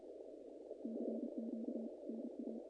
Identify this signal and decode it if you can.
Morse code, heard as a keyed tone, about 33 wpm, 220 Hz. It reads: CQ NA
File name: contests / naqp-cw-k5tr-op-k5pi- -2100